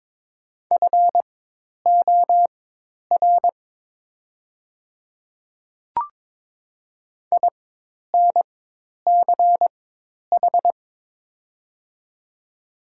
Morse code, heard as a keyed tone, about 22 wpm, spaced as for 10 wpm, 700 Hz. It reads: FOR INCH
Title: 4x Character Spacing - Top 500 Words - 22wpm